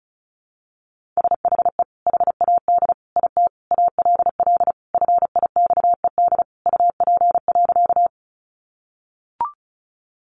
Morse code, heard as a keyed tone, about 34 wpm, 700 Hz. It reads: SHE HAD IT ALL FIXED UP.
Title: Sentences from Top 1100 Words 35wpm